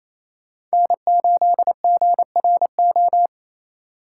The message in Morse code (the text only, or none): N8GRO